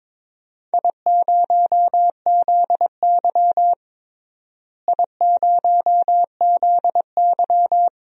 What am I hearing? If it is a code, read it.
Morse code, heard as a keyed tone, about 22 wpm, 700 Hz. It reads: I0ZY I0ZY
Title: DXCC-100 Call Signs - 4 Characters - Repeated 2x - 22wpm